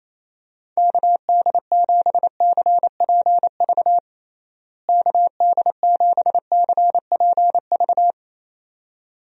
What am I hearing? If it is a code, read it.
Morse code, heard as a keyed tone, about 28 wpm, 700 Hz. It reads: KD7CPV KD7CPV
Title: USA Call Signs - 6 Characters - Repeated 2x - 28wpm